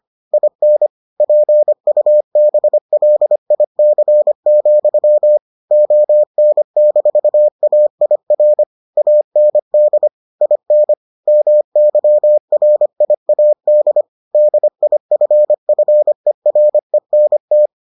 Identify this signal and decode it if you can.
Morse code, heard as a keyed tone, about 25 wpm, 600 Hz. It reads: IN PUBLIC, ON-AIR AND IN MYRIAD DIFFERENT